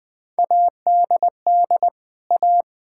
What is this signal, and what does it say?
Morse code, heard as a keyed tone, about 19 wpm, 700 Hz. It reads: ADD A
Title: Sentences with 4-Word Phrases - 20wpm